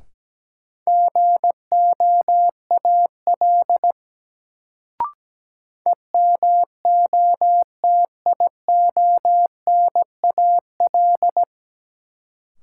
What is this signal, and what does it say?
Morse code, heard as a keyed tone, about 17 wpm, 700 Hz. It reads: GOAL EMOTIONAL
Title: Sentences from Top 1000 Words - Review Part 5 - New Word Drill - 17wpm